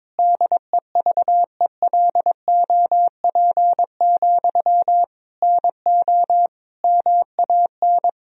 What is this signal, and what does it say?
Morse code, heard as a keyed tone, about 22 wpm, 700 Hz. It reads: DEVELOP, NO MAN
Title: Sentences from Top 1000 Words - Review Part 2 - 22wpm